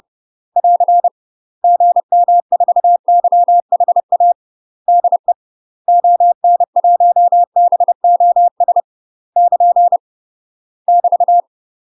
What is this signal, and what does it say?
Morse code, heard as a keyed tone, about 30 wpm, 700 Hz. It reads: <AR> GM4YHA DE ON1BOS <KN> <BT>
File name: UK MASTER QSOs TRAINING / 30WPM / UK MASTER QSOs 0040